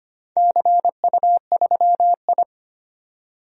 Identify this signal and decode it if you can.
Morse code, heard as a keyed tone, about 25 wpm, 700 Hz. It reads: CU3I
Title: DXCC-100 Call Signs - 4 Characters - 25wpm